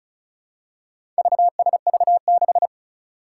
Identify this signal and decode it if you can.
Morse code, heard as a keyed tone, about 36 wpm, 700 Hz. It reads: VSV6
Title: Four Challenging Letter-Number Combinations 35wpm